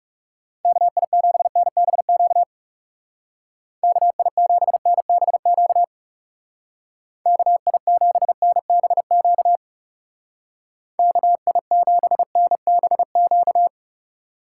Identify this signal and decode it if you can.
Morse code, heard as a keyed tone, about 36 wpm, 700 Hz. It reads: KI7NBQ KI7NBQ KI7NBQ KI7NBQ
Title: Speed Racing - USA 2x3 Call Signs 30wpm